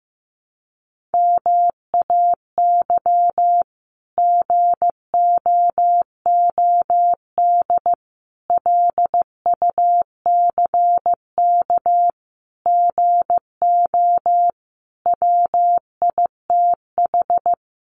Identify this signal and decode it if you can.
Morse code, heard as a keyed tone, about 15 wpm, 700 Hz. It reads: MAY GOOD LUCK GO WITH
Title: Sentences from Top 1500 Words 15wpm